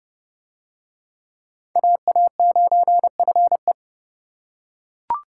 Morse code, heard as a keyed tone, about 29 wpm, 700 Hz. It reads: AA9FE